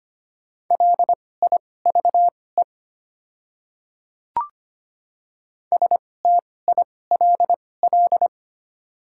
Morse code, heard as a keyed tone, about 25 wpm, spaced as for 18 wpm, 700 Hz. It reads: LIVE STILL
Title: 2x Character Spacing - Top 500 Words - 25wpm